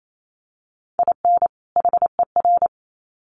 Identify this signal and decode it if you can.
Morse code, heard as a keyed tone, about 28 wpm, 700 Hz. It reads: IN HER
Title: Sets of 2 Words 28wpm